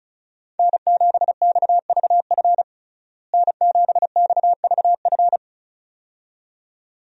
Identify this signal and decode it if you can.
Morse code, heard as a keyed tone, about 34 wpm, 700 Hz. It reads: N7XVF N7XVF
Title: USA Call Signs - 5 Characters - Repeated 2x - 35wpm